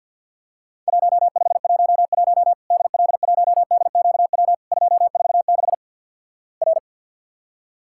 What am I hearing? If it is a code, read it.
Morse code, heard as a keyed tone, about 50 wpm, 700 Hz. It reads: 1511 DL1DQW 246 R
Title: Worked All Europe DX Contest - QTC Exchange - 50wpm